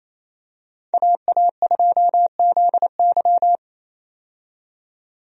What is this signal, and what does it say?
Morse code, heard as a keyed tone, about 28 wpm, 700 Hz. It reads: AA2ZY
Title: USA Call Signs - 5 Characters - 28wpm